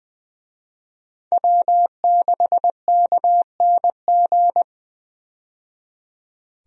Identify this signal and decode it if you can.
Morse code, heard as a keyed tone, about 20 wpm, 700 Hz. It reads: W6KNG
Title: USA Call Signs - 5 Characters - 20wpm